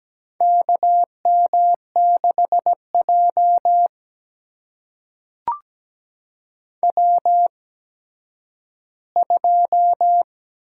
Morse code, heard as a keyed tone, about 17 wpm, spaced as for 10 wpm, 700 Hz. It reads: KM6J W2
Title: USA Call Signs - 8x Character Spacing - 17wpm